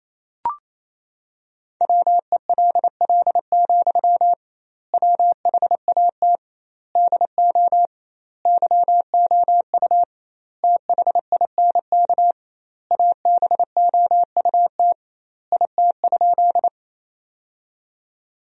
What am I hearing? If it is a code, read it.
Morse code, heard as a keyed tone, about 28 wpm, 700 Hz. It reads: WELL, WHAT DO YOU THINK ABOUT IT?
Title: 2x Word-Spacing - Sentences from Top 200 Words - 28wpm